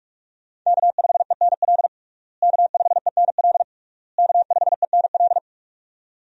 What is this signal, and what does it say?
Morse code, heard as a keyed tone, about 46 wpm, 700 Hz. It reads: K5ENL K5ENL K5ENL
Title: USA Call Signs - 5 Characters - Repeated 3x - 45wpm